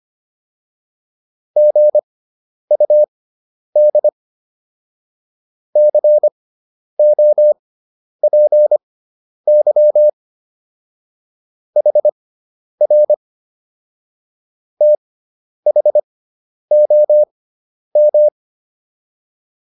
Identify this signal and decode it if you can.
Morse code, heard as a keyed tone, about 25 wpm, spaced as for 10 wpm, 600 Hz. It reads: GUD COPY HR THOM
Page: intermediate-practice-files